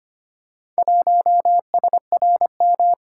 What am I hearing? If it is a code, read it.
Morse code, heard as a keyed tone, about 25 wpm, 690 Hz. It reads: 1SRM
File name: practice / 25wpm / Four Letter-Number Combinations - 25wpm